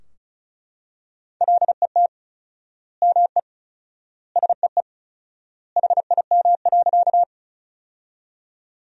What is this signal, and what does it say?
Morse code, heard as a keyed tone, about 34 wpm, 700 Hz. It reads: LET ME SEE HIM.
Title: 4x Word-Spacing - Sentences from Top 300 Words - 35wpm